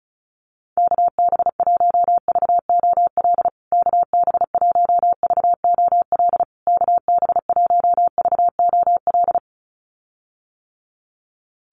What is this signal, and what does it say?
Morse code, heard as a keyed tone, about 34 wpm, 700 Hz. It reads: KB1VOL KB1VOL KB1VOL
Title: USA Call Signs Repeated 3x 35wpm